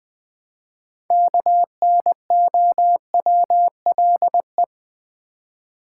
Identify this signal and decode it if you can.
Morse code, heard as a keyed tone, about 20 wpm, 700 Hz. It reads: KNOWLE
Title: Build A Word - 3 Letters At A Time - 9-Letter Words - 20wpm